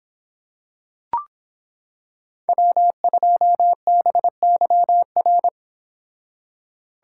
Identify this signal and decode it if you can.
Morse code, heard as a keyed tone, about 26 wpm, 700 Hz. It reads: W2BYR